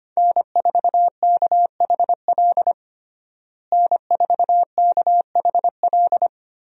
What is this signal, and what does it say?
Morse code, heard as a keyed tone, about 25 wpm, 700 Hz. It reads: N4KHL N4KHL
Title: USA Call Signs - 5 Characters - Repeated 2x - 25wpm